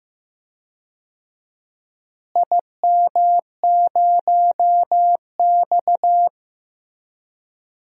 Morse code, heard as a keyed tone, about 15 wpm, 700 Hz. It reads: IM0X
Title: DXCC-100 Call Signs - 4 Characters - Repeated 3x - 15wpm